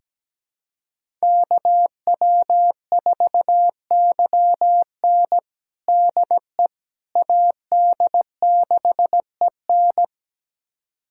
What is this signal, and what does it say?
Morse code, heard as a keyed tone, about 17 wpm, 700 Hz. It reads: KW4YN DE AD6EN
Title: USA Callsign DE Callsign - 5 Characters - 17wpm